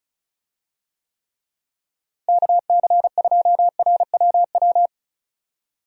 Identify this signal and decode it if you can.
Morse code, heard as a keyed tone, about 36 wpm, 700 Hz. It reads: KC2RWW
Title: USA Call Signs - 6 Characters - Repeated 2x - 35wpm